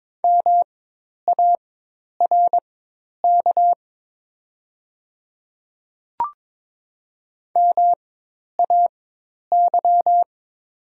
Morse code, heard as a keyed tone, about 22 wpm, spaced as for 10 wpm, 700 Hz. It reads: MARK MAY
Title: 4x Character Spacing - Top 300 Words - 22wpm